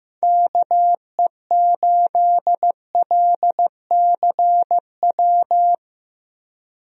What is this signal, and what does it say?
Morse code, heard as a keyed tone, about 15 wpm, 700 Hz. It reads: KE8LCW
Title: USA Call Signs - 6 Characters - Repeated 2x - 15wpm